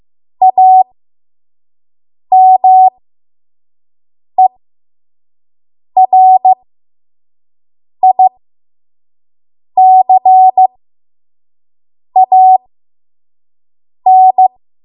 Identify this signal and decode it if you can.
Morse code, heard as a keyed tone, about 15 wpm, spaced as for 5 wpm, 750 Hz. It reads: AMERICAN
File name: Morse / 05WPM